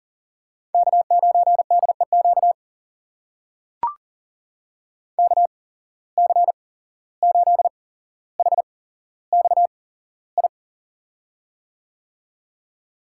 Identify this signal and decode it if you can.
Morse code, heard as a keyed tone, about 42 wpm, spaced as for 16 wpm, 700 Hz. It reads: K9DEQ KC8HXI
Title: USA Call Signs - 8x Character Spacing - 40wpm